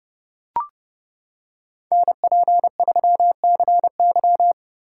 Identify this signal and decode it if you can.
Morse code, heard as a keyed tone, about 30 wpm, 700 Hz. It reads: NP3CY